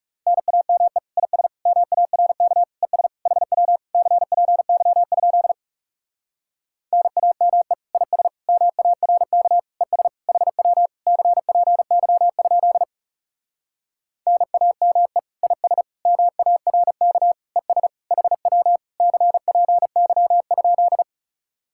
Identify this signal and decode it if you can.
Morse code, heard as a keyed tone, about 40 wpm, 700 Hz. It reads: NAME IS MARK ES HW CPY? NAME IS MARK ES HW CPY? NAME IS MARK ES HW CPY?